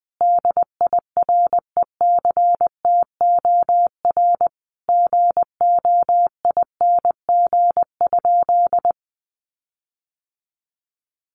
Morse code, heard as a keyed tone, about 20 wpm, 700 Hz. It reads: DIRECTOR GOING?
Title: Sentences from Top 1300 Words 20wpm